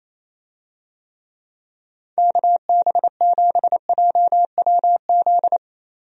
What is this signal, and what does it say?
Morse code, heard as a keyed tone, about 28 wpm, 700 Hz. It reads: KB7JWZ
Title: USA Call Signs - 6 Characters - 28wpm